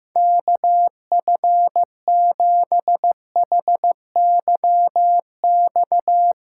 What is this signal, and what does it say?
Morse code, heard as a keyed tone, about 15 wpm, 700 Hz. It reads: KF7HYX
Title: USA Callsign DE Callsign - 6 Characters - 15wpm